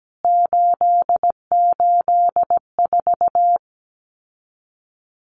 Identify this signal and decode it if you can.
Morse code, heard as a keyed tone, about 17 wpm, 700 Hz. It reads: 884
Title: CWT Exchange 17wpm